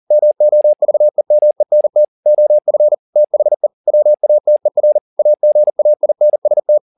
Morse code, heard as a keyed tone, about 40 wpm, 600 Hz. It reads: MOVEMENT OF THE WATER AGAINST
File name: the old man and the sea 2 40wpm